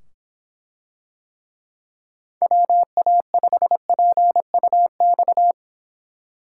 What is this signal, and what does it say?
Morse code, heard as a keyed tone, about 25 wpm, 700 Hz. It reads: WA5PUX